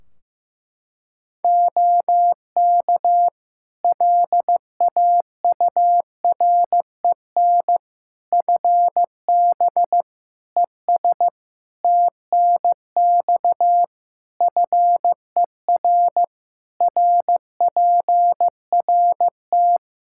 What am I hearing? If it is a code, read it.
Morse code, heard as a keyed tone, about 16 wpm, 700 Hz. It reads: OK LAUREN FB ES TNX FER RPRT
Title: Practice QSO - Exchange 2 - Info - 15wpm